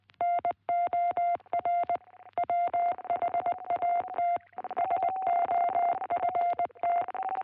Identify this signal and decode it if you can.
Morse code, heard as a keyed tone, about 18 wpm, 660 Hz. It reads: NOR WHAT SOFT